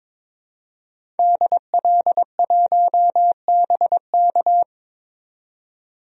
Morse code, heard as a keyed tone, about 22 wpm, 700 Hz. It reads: DL1BK
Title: DXCC-100 Call Signs - 5 Characters - 22wpm